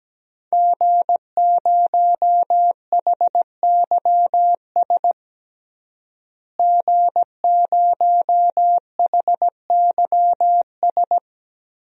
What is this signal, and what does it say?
Morse code, heard as a keyed tone, about 17 wpm, 700 Hz. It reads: G0HYS G0HYS
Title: DXCC-100 Call Signs - 5 Characters - Repeated 3x - 17wpm